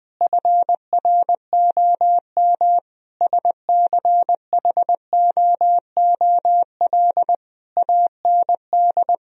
Morse code, heard as a keyed tone, about 20 wpm, 700 Hz. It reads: FROM SCHOOL AND
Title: Sentences from Top 1000 Words - Review Part 4 - 20wpm